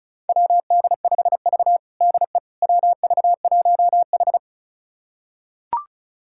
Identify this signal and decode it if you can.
Morse code, heard as a keyed tone, about 34 wpm, 700 Hz. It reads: WD5V DE WV1H
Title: USA Callsign DE Callsign - 4 Characters - 35wpm